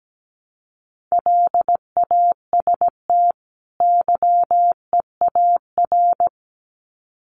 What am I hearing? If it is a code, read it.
Morse code, heard as a keyed tone, about 17 wpm, 700 Hz. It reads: LAST YEAR
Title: Sets of 2 Words 17wpm